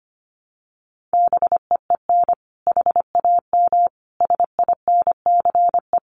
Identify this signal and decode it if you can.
Morse code, heard as a keyed tone, about 25 wpm, 700 Hz. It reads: BEEN HAM SINCE
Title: Common QSO Phrases 25wpm